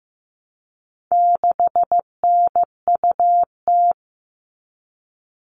Morse code, heard as a keyed tone, about 15 wpm, 700 Hz. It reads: 6NUT
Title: All Cut Numbers - 4-Digit 15wpm